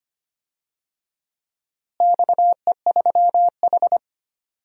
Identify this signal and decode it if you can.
Morse code, heard as a keyed tone, about 25 wpm, 700 Hz. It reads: XE3H
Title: DXCC-100 Call Signs - 4 Characters - Repeated 2x - 25wpm